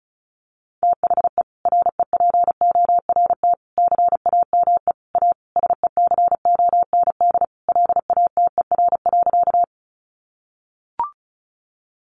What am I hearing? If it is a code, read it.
Morse code, heard as a keyed tone, about 34 wpm, 700 Hz. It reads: THE REPORT CAME A SECOND LATER.
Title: Sentences from Top 900 Words 35wpm